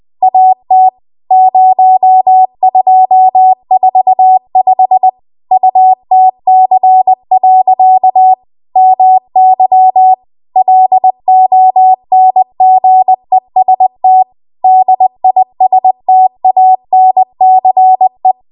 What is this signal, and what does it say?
Morse code, heard as a keyed tone, about 20 wpm, 750 Hz. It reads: AT 0245 UTC. MY LONGEST DISTANCE
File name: Archive / 20 WPM / 20WPM